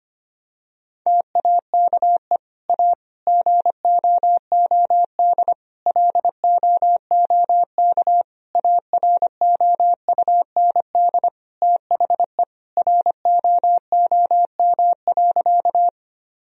Morse code, heard as a keyed tone, about 25 wpm, 700 Hz. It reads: TAKE A GOOD LOOK AROUND THE ROOM.
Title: Sentences from Top 1000 Words - Review Part 3 - 25wpm